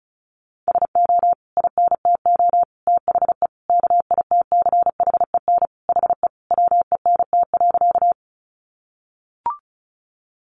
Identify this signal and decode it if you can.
Morse code, heard as a keyed tone, about 34 wpm, 700 Hz. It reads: SO INTO THE KITCHEN HE WENT.
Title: Sentences from Top 1400 Words 35wpm